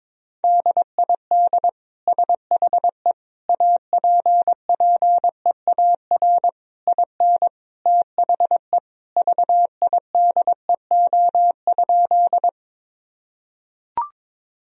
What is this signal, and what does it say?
Morse code, heard as a keyed tone, about 21 wpm, 700 Hz. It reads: DID SHE APPEAR IN THE VIDEO?
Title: Sentences from Top 1000 Words - Review Part 4 - 22wpm